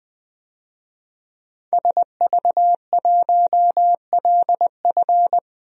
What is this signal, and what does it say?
Morse code, heard as a keyed tone, about 20 wpm, 700 Hz. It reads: SV1LF
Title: DXCC-100 Call Signs - 5 Characters - Repeated 3x - 20wpm